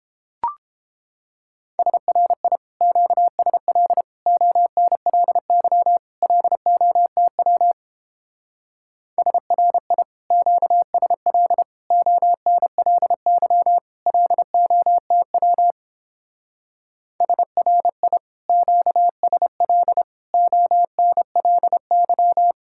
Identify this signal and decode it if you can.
Morse code, heard as a keyed tone, about 29 wpm, 700 Hz. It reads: SRI QSL ONLY LOTW SRI QSL ONLY LOTW SRI QSL ONLY